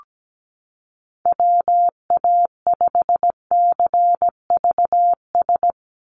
Morse code, heard as a keyed tone, about 17 wpm, 700 Hz. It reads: WA5CVS
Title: USA Call Signs 17wpm